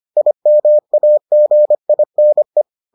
Morse code, heard as a keyed tone, about 25 wpm, 600 Hz. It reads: IMAGINE